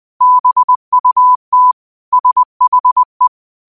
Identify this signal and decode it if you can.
Morse code, heard as a keyed tone, about 20 wpm, 1000 Hz. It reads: BUT SHE
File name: qotdfiles / 20wpm